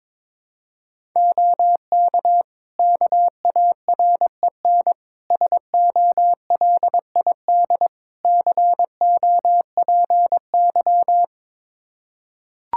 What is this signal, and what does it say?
Morse code, heard as a keyed tone, about 22 wpm, 700 Hz. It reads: OK KAREN SOLID COPY
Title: Practice QSO - Exchange 3 - More Info - 22wpm